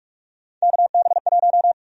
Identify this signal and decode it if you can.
Morse code, heard as a keyed tone, about 41 wpm, 700 Hz. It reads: KB1
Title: USA Call Sign Prefixes - 45wpm